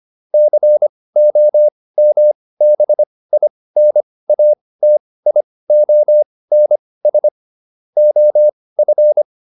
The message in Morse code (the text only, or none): COMBINATIONS OF